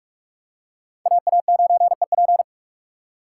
Morse code, heard as a keyed tone, about 45 wpm, 700 Hz. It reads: AA9EP
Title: USA Call Signs - 5 Characters - 6x Character Spacing - 45wpm